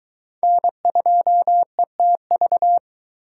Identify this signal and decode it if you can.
Morse code, heard as a keyed tone, about 19 wpm, 700 Hz. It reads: N2ETV